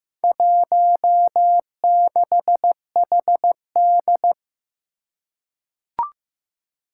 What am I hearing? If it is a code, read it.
Morse code, heard as a keyed tone, about 15 wpm, 700 Hz. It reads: 16HD
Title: Four Challenging Letter-Number Combinations 15wpm